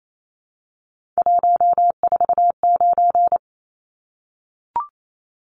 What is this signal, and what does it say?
Morse code, heard as a keyed tone, about 28 wpm, 700 Hz. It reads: 149
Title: Numbers - 3-Digit 28wpm